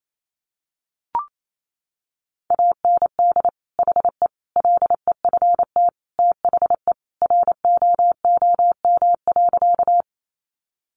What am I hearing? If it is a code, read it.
Morse code, heard as a keyed tone, about 28 wpm, 700 Hz. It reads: AND HE LEFT THE ROOM.
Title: Sentences from Top 300 Words 28wpm